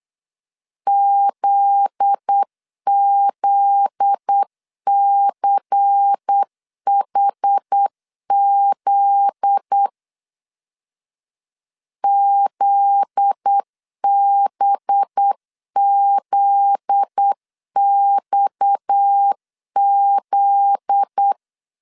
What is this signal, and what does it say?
Morse code, heard as a keyed tone, about 8 wpm, 800 Hz. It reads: ZZCHZ ZBZXZ
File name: LEZIONI / 7 GF9Z / 7 GF9Z 8wpm 3213